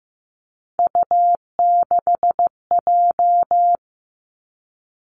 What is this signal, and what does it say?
Morse code, heard as a keyed tone, about 15 wpm, 700 Hz. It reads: U6J
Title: Three Challenging Letter-Number Combinations 15wpm